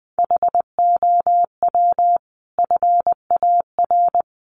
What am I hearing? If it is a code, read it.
Morse code, heard as a keyed tone, about 20 wpm, 700 Hz. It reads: HOW FAR
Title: Sentences from Top 400 Words 20wpm